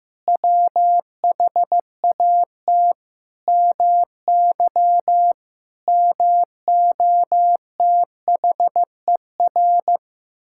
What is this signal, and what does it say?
Morse code, heard as a keyed tone, about 15 wpm, 700 Hz. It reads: WHAT MY MOTHER
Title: Sentences from Top 1000 Words - Review Part 6 - 15wpm